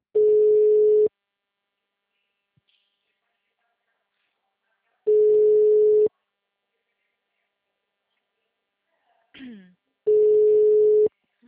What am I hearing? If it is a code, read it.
Ringback tone (Europe, 425 Hz).
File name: out- -2029- - - .80941